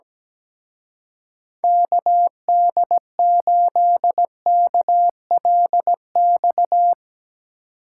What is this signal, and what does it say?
Morse code, heard as a keyed tone, about 17 wpm, 700 Hz. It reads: KD8KLX
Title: Speed Racing - USA 2x3 Call Signs 17wpm